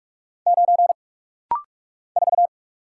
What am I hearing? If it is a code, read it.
Morse code, heard as a keyed tone, about 46 wpm, 700 Hz. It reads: 9 4
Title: Single Letter-Number Alt - Rapid-Fire 45wpm